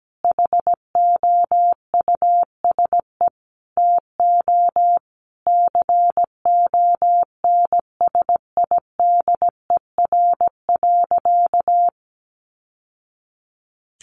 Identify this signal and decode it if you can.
Morse code, heard as a keyed tone, about 17 wpm, 700 Hz. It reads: HOUSE TO CONSIDER.
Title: Sentences from Top 900 Words 17wpm